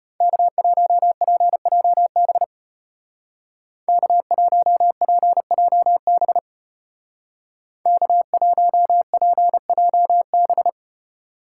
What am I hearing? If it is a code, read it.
Morse code, heard as a keyed tone, about 30 wpm, 700 Hz. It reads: K1PJB K1PJB K1PJB
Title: Speed Racing - USA 1x3 Call Signs 25wpm